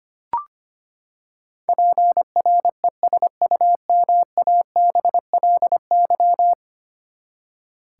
Morse code, heard as a keyed tone, about 25 wpm, 700 Hz. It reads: PRESUMABLY
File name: practice / 25wpm / Ten Letter Words - 25wpm